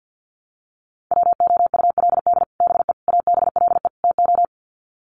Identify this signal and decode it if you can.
Morse code, heard as a keyed tone, about 48 wpm, 700 Hz. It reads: WOULD BE ABLE TO
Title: Sets of 4 Words 50wpm